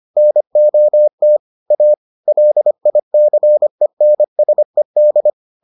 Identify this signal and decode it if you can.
Morse code, heard as a keyed tone, about 25 wpm, 600 Hz. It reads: NOT A LICENSED